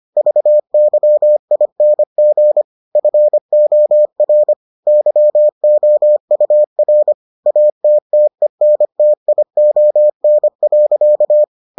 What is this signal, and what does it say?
Morse code, heard as a keyed tone, about 25 wpm, 600 Hz. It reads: VYING FOR YOUR ATTENTION.